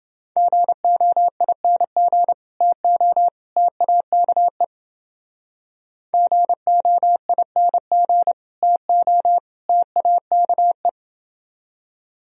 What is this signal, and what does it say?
Morse code, heard as a keyed tone, about 31 wpm, 700 Hz. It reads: GOING TO TAKE GOING TO TAKE